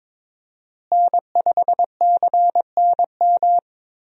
Morse code, heard as a keyed tone, about 22 wpm, 700 Hz. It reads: N5CNM
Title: USA Call Signs - 5 Characters - Repeated 2x - 22wpm